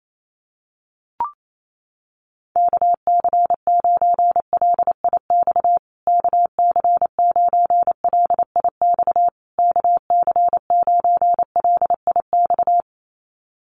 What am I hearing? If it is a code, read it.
Morse code, heard as a keyed tone, about 28 wpm, 700 Hz. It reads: KC9LIX KC9LIX KC9LIX
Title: USA Call Signs Repeated 3x 28wpm